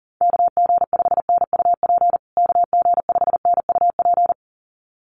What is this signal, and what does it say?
Morse code, heard as a keyed tone, about 40 wpm, 700 Hz. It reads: KG5NUP KG5NUP
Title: USA Call Signs Repeated 2x 40wpm